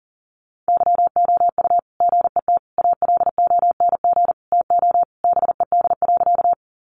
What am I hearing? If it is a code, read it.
Morse code, heard as a keyed tone, about 40 wpm, 700 Hz. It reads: YOU GET ALONG TO BED.
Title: Sentences from Top 700 Words 40wpm